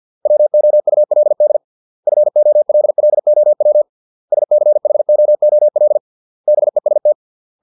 Morse code, heard as a keyed tone, about 50 wpm, 600 Hz. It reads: WOULD FOLLOW SCHOOL BEST